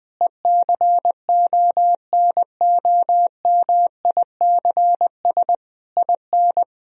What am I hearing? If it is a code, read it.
Morse code, heard as a keyed tone, about 20 wpm, 700 Hz. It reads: ECONOMICS IN
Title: Sentences from Top 1000 Words - Review Part 3 - 20wpm